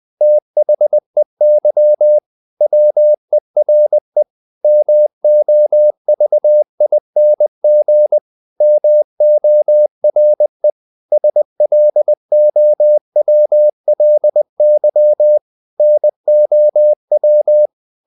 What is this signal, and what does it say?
Morse code, heard as a keyed tone, about 20 wpm, 600 Hz. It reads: THEY WERE MOVING MORE SLOWLY NOW
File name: the old man and the sea / the old man and the sea part 1 20wpm